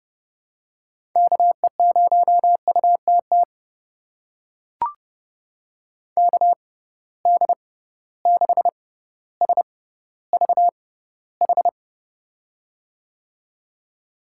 Morse code, heard as a keyed tone, about 30 wpm, spaced as for 15 wpm, 700 Hz. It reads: KE0UTT KD6SVH
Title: USA Call Signs - 6 Characters - 6x Character Spacing - 30wpm